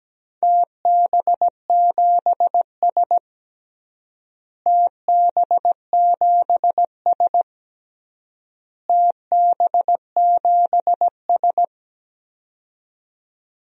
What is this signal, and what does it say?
Morse code, heard as a keyed tone, about 17 wpm, 700 Hz. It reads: TB7S TB7S TB7S
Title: DXCC-100 Call Signs - 4 Characters - Repeated 3x - 17wpm